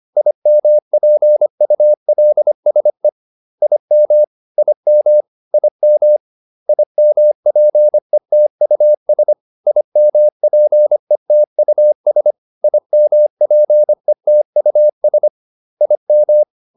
Morse code, heard as a keyed tone, about 25 wpm, 600 Hz. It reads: IMPULSE IM IM IM IMPETUS IMPETUS IMPETUS IM